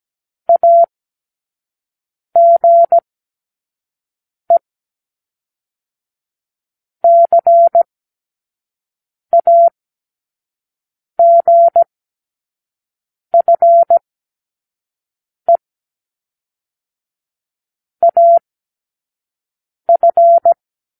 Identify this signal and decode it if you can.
Morse code, heard as a keyed tone, about 17 wpm, 680 Hz. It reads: A G E C A G F E A F
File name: k7qo / Lesson12(A-H)